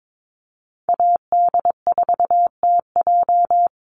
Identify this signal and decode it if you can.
Morse code, heard as a keyed tone, about 22 wpm, 700 Hz. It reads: AD4TJ
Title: Call Signs Repeated 3x - 22wpm